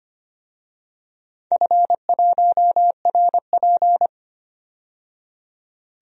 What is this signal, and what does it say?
Morse code, heard as a keyed tone, about 25 wpm, 700 Hz. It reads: F1RP
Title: DXCC-100 Call Signs - 4 Characters - 25wpm